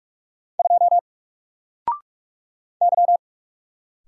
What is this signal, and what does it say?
Morse code, heard as a keyed tone, about 44 wpm, 700 Hz. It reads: 2 Y
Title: Single Character (Letter:Number:Punctuation:Prosign) 45wpm